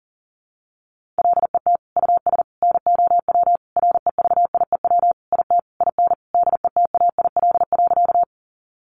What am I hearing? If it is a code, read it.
Morse code, heard as a keyed tone, about 37 wpm, 700 Hz. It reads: LET US NOW REVIEW IT IN DETAIL.
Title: Sentences from Top 1500 Words 40wpm